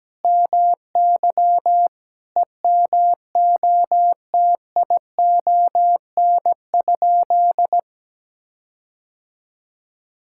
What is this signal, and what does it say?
Morse code, heard as a keyed tone, about 17 wpm, 700 Hz. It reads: MY EMOTION?
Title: Sentences from Top 1600 Words 17wpm